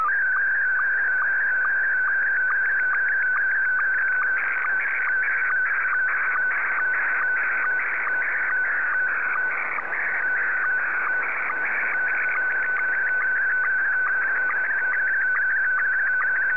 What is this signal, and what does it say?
SSTV (slow-scan television), Scottie 1.